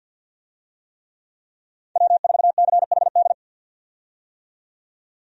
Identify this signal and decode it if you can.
Morse code, heard as a keyed tone, about 51 wpm, 700 Hz. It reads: W4CHD
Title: USA Call Signs - 5 Characters - 50wpm